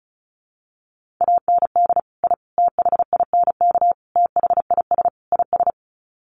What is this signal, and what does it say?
Morse code, heard as a keyed tone, about 34 wpm, 700 Hz. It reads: AND I THINK THIS IS
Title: Sets of 5 Words Encore 35wpm